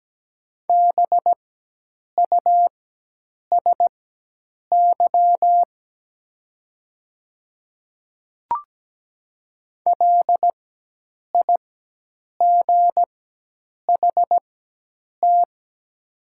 Morse code, heard as a keyed tone, about 17 wpm, spaced as for 8 wpm, 700 Hz. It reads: BUSY LIGHT
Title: 4x Character Spacing - Top 500 Words - 17wpm